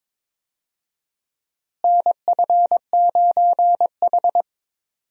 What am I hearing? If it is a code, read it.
Morse code, heard as a keyed tone, about 22 wpm, 700 Hz. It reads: NF9H
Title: USA Call Signs - 4 Characters - Repeated 2x - 22wpm